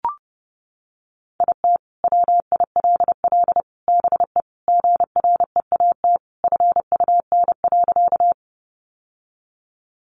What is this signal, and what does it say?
Morse code, heard as a keyed tone, about 30 wpm, 700 Hz. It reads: IT WILL BE GREAT FUN.
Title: Sentences from Top 700 Words 30wpm